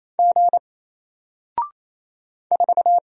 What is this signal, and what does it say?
Morse code, heard as a keyed tone, about 28 wpm, 700 Hz. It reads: G 4
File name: Single-Letter-Number-28wpm